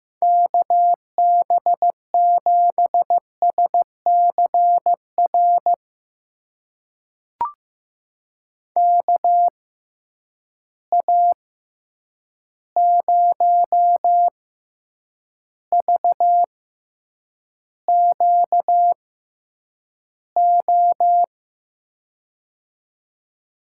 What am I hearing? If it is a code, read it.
Morse code, heard as a keyed tone, about 15 wpm, spaced as for 8 wpm, 700 Hz. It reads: KB7SCR KA0VQO
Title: USA Call Signs - 6x Character Spacing - 15wpm